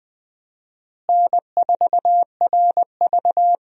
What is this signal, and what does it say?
Morse code, heard as a keyed tone, about 20 wpm, 700 Hz. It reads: N4RV
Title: USA Call Signs - 4 Characters - Repeated 3x - 20wpm